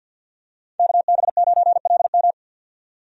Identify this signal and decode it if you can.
Morse code, heard as a keyed tone, about 46 wpm, 700 Hz. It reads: KB9LM
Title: USA Call Signs - 5 Characters - 50wpm